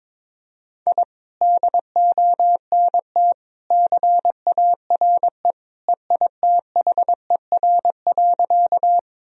Morse code, heard as a keyed tone, about 22 wpm, 700 Hz. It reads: I DONT CARE EITHER.
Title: Sentences from Top 1000 Words - Review Part 4 - 22wpm